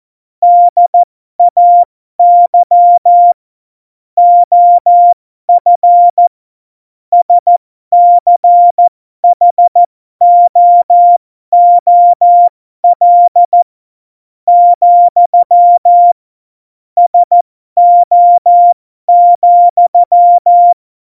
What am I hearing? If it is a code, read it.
Morse code, heard as a keyed tone, about 14 wpm, spaced as for 11 wpm, 700 Hz. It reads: DAY OF SCHOOL, SO,
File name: Morsekurs / Witze